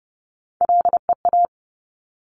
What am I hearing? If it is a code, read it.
Morse code, heard as a keyed tone, about 30 wpm, 700 Hz. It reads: LEA
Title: Top Three Letter Combinations in Words 30wpm